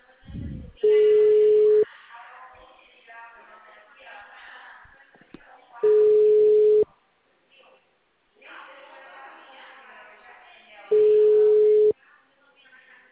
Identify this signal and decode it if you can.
Ringback tone (Europe, 425 Hz), with speech.